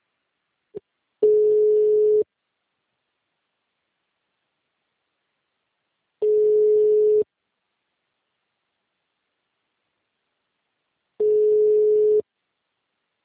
Ringback tone (Europe, 425 Hz).